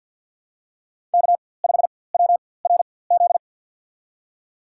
Morse code, heard as a keyed tone, about 50 wpm, spaced as for 23 wpm, 700 Hz. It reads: K5WRZ
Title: USA Call Signs - 5 Characters - 4x Character Spacing - 50wpm